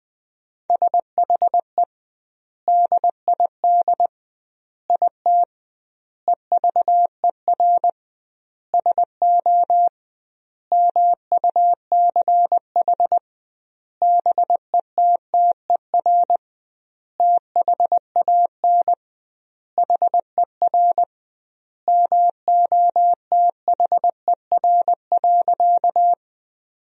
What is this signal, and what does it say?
Morse code, heard as a keyed tone, about 20 wpm, 700 Hz. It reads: SHE DID IT EVER SO MUCH BETTER THAN HER MOTHER.
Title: 2x Word-Spacing - Sentences from Top 400 Words - 20wpm